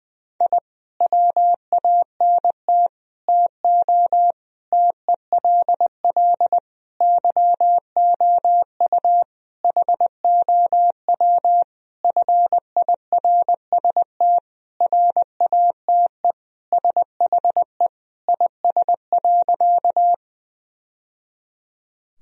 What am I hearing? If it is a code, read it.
Morse code, heard as a keyed tone, about 20 wpm, 700 Hz. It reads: I WANT TO TELL YOU HOW FIRST RATE SHE IS.
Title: Sentences from Top 1000 Words - Review Part 4 - 20wpm